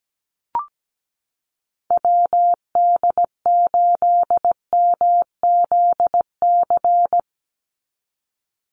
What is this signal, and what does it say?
Morse code, heard as a keyed tone, about 17 wpm, 700 Hz. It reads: WD8MZC
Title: USA Call Signs 17wpm